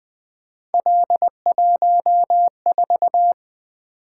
Morse code, heard as a keyed tone, about 20 wpm, 700 Hz. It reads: L14
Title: Three Letter-Number Combinations - 20wpm